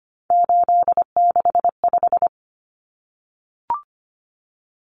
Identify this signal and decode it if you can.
Morse code, heard as a keyed tone, about 25 wpm, 700 Hz. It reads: 865
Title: Common Cut Numbers - 3-Digit 25wpm